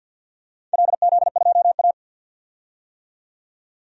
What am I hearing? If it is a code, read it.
Morse code, heard as a keyed tone, about 50 wpm, 700 Hz. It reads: LZ2A
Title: DXCC-100 Call Signs - 4 Characters - 50wpm